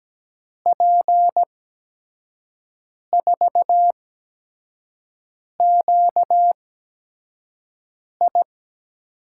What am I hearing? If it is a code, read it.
Morse code, heard as a keyed tone, about 17 wpm, spaced as for 5 wpm, 700 Hz. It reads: P4QI
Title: USA Call Signs - 5 Characters - 8x Character Spacing - 17wpm